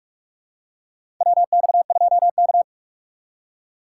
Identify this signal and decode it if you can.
Morse code, heard as a keyed tone, about 44 wpm, 700 Hz. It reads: WX2K
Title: USA Call Signs - 6x Character Spacing - 45wpm